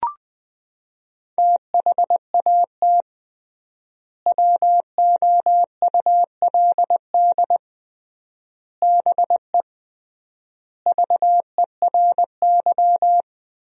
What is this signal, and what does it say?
Morse code, heard as a keyed tone, about 21 wpm, 700 Hz. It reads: THAT WOULD BE VERY
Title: 3x Word-Spacing - Sentences from Top 200 Words - 20wpm